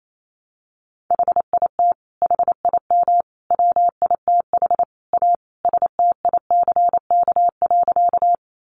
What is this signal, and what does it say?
Morse code, heard as a keyed tone, about 28 wpm, 700 Hz. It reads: HIT HIM WITH A STICK.
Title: Sentences from Top 1200 Words 28wpm